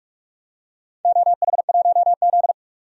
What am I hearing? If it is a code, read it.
Morse code, heard as a keyed tone, about 46 wpm, 700 Hz. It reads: OH1Z
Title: DXCC-100 Call Signs - 4 Characters - 45wpm